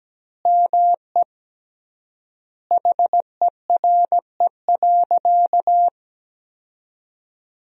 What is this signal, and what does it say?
Morse code, heard as a keyed tone, about 17 wpm, 700 Hz. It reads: ME HERE.
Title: 3x Word-Spacing - Sentences from Top 200 Words - 17wpm